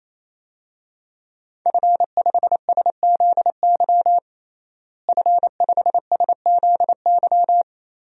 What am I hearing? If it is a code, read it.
Morse code, heard as a keyed tone, about 28 wpm, 700 Hz. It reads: F5SZY F5SZY
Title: DXCC-100 Call Signs - 5 Characters - Repeated 3x - 28wpm